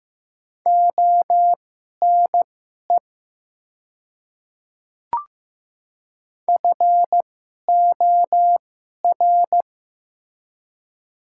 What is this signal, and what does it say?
Morse code, heard as a keyed tone, about 15 wpm, spaced as for 11 wpm, 690 Hz. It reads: ONE FOR
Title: 2x Character Spacing - Top 100 Words - 15wpm